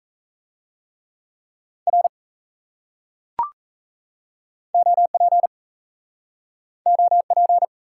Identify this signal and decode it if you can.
Morse code, heard as a keyed tone, about 39 wpm, 700 Hz. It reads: R OP OP